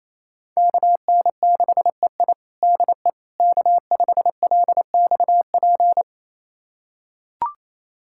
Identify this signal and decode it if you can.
Morse code, heard as a keyed tone, about 28 wpm, 700 Hz. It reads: KN6EI DE K5LXP
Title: USA Callsign DE Callsign - 5 Characters - 28wpm